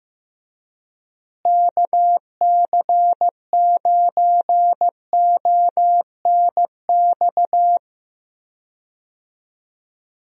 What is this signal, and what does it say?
Morse code, heard as a keyed tone, about 15 wpm, 700 Hz. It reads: KC9ONX
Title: USA Call Signs - 6 Characters - Repeated 3x - 15wpm